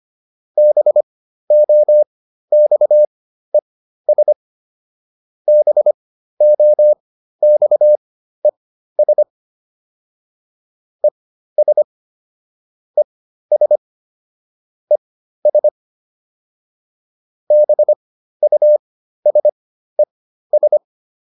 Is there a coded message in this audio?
Yes: Morse code from 0.6 to 20.8 s, heard as a keyed tone, about 25 wpm, spaced as for 13 wpm, 600 Hz. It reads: BOXES BOXES ES ES ES BUSES